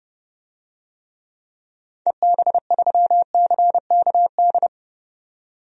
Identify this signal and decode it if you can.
Morse code, heard as a keyed tone, about 31 wpm, 700 Hz. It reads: EB3CKD
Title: DXCC-100 Call Signs - 6 Characters - 30wpm